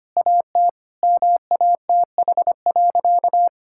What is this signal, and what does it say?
Morse code, heard as a keyed tone, about 25 wpm, 700 Hz. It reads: AT MATH.
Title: Sentences from Top 1000 Words - Review Part 6 - 25wpm